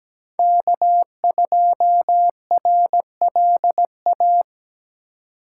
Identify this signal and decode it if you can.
Morse code, heard as a keyed tone, about 17 wpm, 700 Hz. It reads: K2RLA
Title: Code Groups - 5 Letter:Numbers - 17wpm